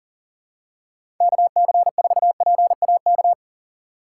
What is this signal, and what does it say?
Morse code, heard as a keyed tone, about 38 wpm, 700 Hz. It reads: KC4PAK
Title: USA Call Signs - 6 Characters - 2x Character Spacing - 40wpm